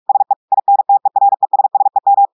Morse code, heard as a keyed tone, about 46 wpm, 800 Hz. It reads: SE INTERESSEM
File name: audios / portugais / portugais texte en clair 2 vitesse 45 mots minutes